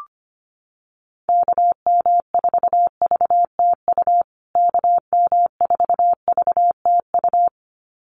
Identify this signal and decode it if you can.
Morse code, heard as a keyed tone, about 25 wpm, 700 Hz. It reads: KM4VTU KM4VTU
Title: USA Call Signs Repeated 2x 25wpm